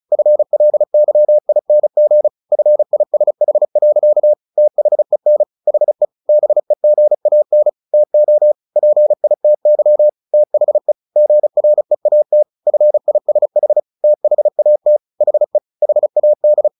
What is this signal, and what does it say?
Morse code, heard as a keyed tone, about 34 wpm, 600 Hz. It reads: FLYING FISH. THEN HE BEGAN TO PITY THE GREAT FISH THAT HE HAD
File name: the old man and the sea / the old man and the sea 1 35wpm